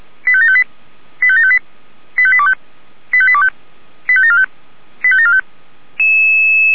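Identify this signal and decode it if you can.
Five-tone selective call (ZVEI), heard as quick runs of five tones, tones of about 80 ms. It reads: 85758 85758 85825 85825 85745 85745
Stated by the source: ZVEI five-tone call 85827 received by SDR.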